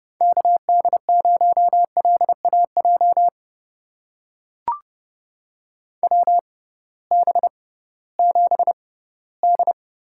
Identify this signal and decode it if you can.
Morse code, heard as a keyed tone, about 31 wpm, spaced as for 18 wpm, 700 Hz. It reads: KD0LAJ WB7D
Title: USA Call Signs - 6 Characters - 6x Character Spacing - 30wpm